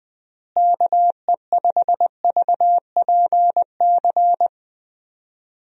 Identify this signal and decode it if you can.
Morse code, heard as a keyed tone, about 20 wpm, 700 Hz. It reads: KE5VPC
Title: USA Call Signs - 6 Characters - Repeated 3x - 20wpm